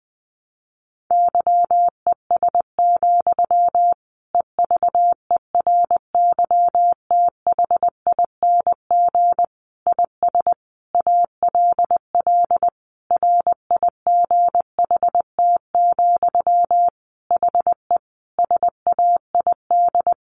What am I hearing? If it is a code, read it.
Morse code, heard as a keyed tone, about 20 wpm, 700 Hz. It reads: YES, EVERYTHING IS ALL RIGHT, HE SAID
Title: Sentences from Top 1100 Words 20wpm